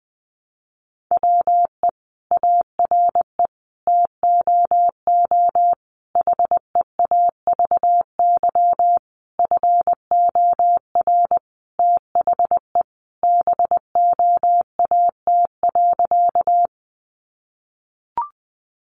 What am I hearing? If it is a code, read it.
Morse code, heard as a keyed tone, about 20 wpm, 700 Hz. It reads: WE ARE TOO HEAVY FOR THE BOAT.
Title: Sentences from Top 1100 Words 20wpm